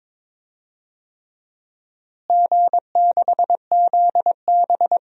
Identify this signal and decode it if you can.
Morse code, heard as a keyed tone, about 22 wpm, 700 Hz. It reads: G6ZB
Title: DXCC-100 Call Signs - 4 Characters - Repeated 3x - 22wpm